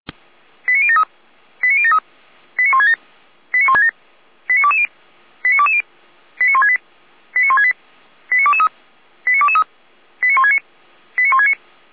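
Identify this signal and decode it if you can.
Five-tone selective call (ZVEI), heard as quick runs of five tones, tones of about 70 ms. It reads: 89073 89073 89167 89167 89220 89220 89168 89168 89223 89223 89169 89169
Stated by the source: ZVEI five-tone selective call received on an SDR.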